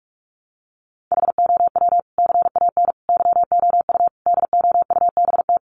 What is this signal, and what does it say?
Morse code, heard as a keyed tone, about 44 wpm, 700 Hz. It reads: HOW CAN YOU DOUBT
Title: Sentences from Top 1500 Words 45wpm